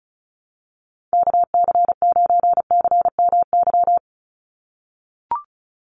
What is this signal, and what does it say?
Morse code, heard as a keyed tone, about 34 wpm, 700 Hz. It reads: KC9CMY
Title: USA Call Signs 35wpm